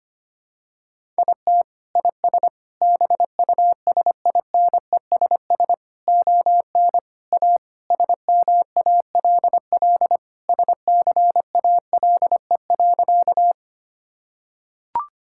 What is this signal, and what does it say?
Morse code, heard as a keyed tone, about 25 wpm, 700 Hz. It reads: IT IS BUSINESS ON A SMALL SCALE.
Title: Sentences from Top 1000 Words - Review Part 5 - 25wpm